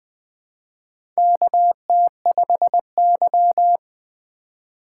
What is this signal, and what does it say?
Morse code, heard as a keyed tone, about 20 wpm, 700 Hz. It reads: KT5Y
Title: USA Call Signs - 4 Characters - 2x Character Spacing - 20wpm